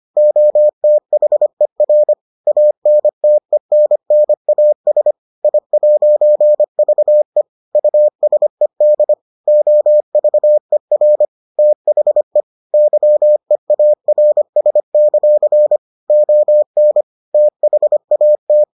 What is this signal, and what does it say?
Morse code, heard as a keyed tone, about 25 wpm, 600 Hz. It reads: OTHER ANTENNAS I'VE USED OVER THE YEARS; ON THAT